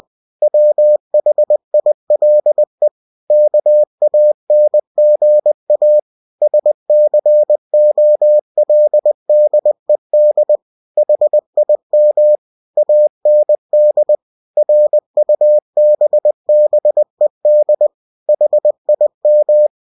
Morse code, heard as a keyed tone, about 20 wpm, 600 Hz. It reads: WHILE KANGA SCOLDED HIM AND RUBBED HIM
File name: cw-books / winnie-the-pooh / winnie-20wpm-page-0087